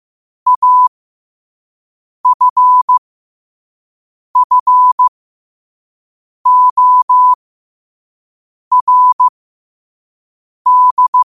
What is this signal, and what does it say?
Morse code, heard as a keyed tone, about 15 wpm, spaced as for 5 wpm, 1000 Hz. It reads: AFFORD
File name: qotdfiles / 5wpm